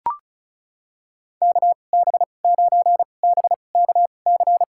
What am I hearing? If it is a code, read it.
Morse code, heard as a keyed tone, about 36 wpm, spaced as for 25 wpm, 700 Hz. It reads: KB9BKC
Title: USA Call Signs - 6 Characters - 2x Character Spacing - 35wpm